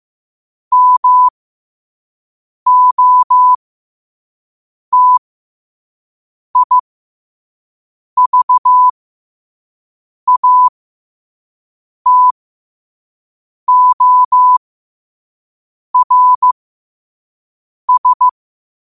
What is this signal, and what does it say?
Morse code, heard as a keyed tone, about 15 wpm, spaced as for 5 wpm, 1000 Hz. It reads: MOTIVATORS
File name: qotdfiles / 5wpm